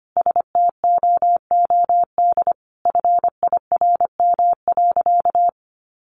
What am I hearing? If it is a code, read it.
Morse code, heard as a keyed tone, about 25 wpm, 700 Hz. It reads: STOOD FIRM.
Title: Sentences from Top 1400 Words 25wpm